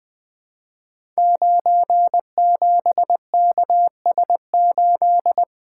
Morse code, heard as a keyed tone, about 20 wpm, 700 Hz. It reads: 97KS8
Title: Code Groups - 5 Letter:Numbers - 20wpm